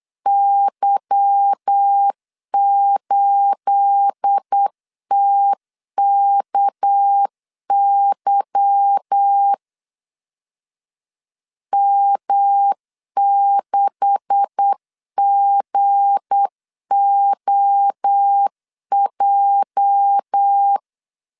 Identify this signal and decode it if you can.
Morse code, heard as a keyed tone, about 8 wpm, 800 Hz. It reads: Y8TKY M6GOJ
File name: CW / DET / 1 9 1500C S A 8wpm 5637